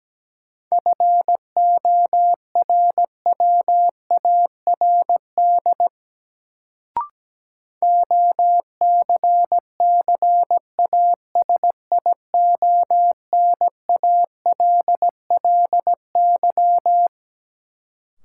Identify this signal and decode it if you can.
Morse code, heard as a keyed tone, about 17 wpm, 700 Hz. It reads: FORWARD OCCASIONALLY
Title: Sentences from Top 1300 Words - New Word Drill Alt - 17wpm